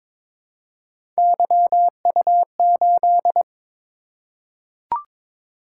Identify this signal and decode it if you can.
Morse code, heard as a keyed tone, about 22 wpm, 700 Hz. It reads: YU8
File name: practice / 22wpm / Three Letter-Number Combinations - 22wpm